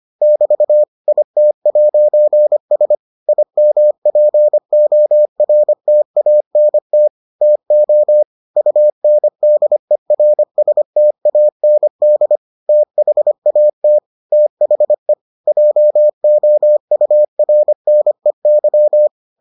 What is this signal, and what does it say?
Morse code, heard as a keyed tone, about 25 wpm, 600 Hz. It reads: <BT> IT'S IMPORTANT TO UNDERSTAND THAT THE JOURNEY